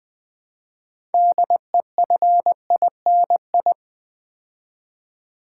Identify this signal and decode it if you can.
Morse code, heard as a keyed tone, about 20 wpm, 700 Hz. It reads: DEFINI
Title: Build A Word - 3 Letters At A Time - 8-Letter Words - 20wpm